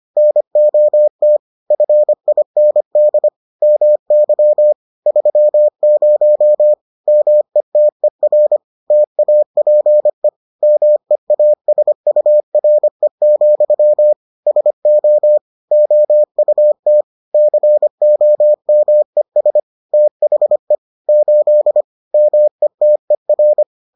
Morse code, heard as a keyed tone, about 25 wpm, 600 Hz. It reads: NOT FIND MY 30 METER TAPE MEASURE, SO OUT COMES THE 8 METER